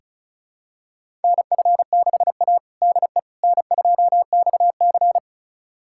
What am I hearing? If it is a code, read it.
Morse code, heard as a keyed tone, about 34 wpm, 700 Hz. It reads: NF6A DE N2XC
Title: USA Callsign DE Callsign - 4 Characters - 35wpm